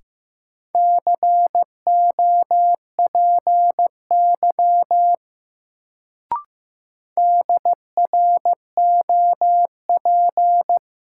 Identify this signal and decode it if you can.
Morse code, heard as a keyed tone, about 15 wpm, 700 Hz. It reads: COPY DROP
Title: Sentences from Top 1400 Words - New Word Drill - 15wpm